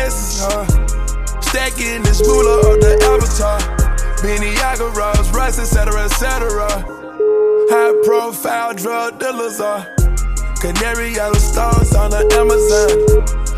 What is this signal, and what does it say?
Ringback tone (Europe, 425 Hz), with noise.